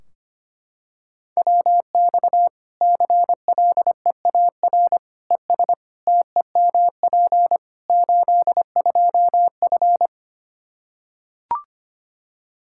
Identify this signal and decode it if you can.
Morse code, heard as a keyed tone, about 25 wpm, 700 Hz. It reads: WX CLEAR ES TEMP 82F
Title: Practice QSO - Exchange 2 - Info - 25wpm